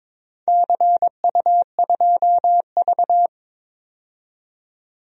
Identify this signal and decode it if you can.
Morse code, heard as a keyed tone, about 21 wpm, 700 Hz. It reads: CU2V
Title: DXCC-100 Call Signs - 4 Characters - 22wpm